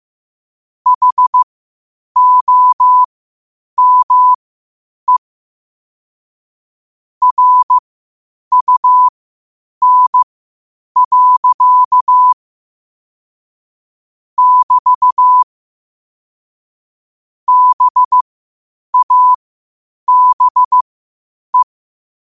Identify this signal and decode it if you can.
Morse code, heard as a keyed tone, about 15 wpm, spaced as for 8 wpm, 1000 Hz. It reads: HOME RUN. <BT> BABE
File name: qotdfiles / 8wpm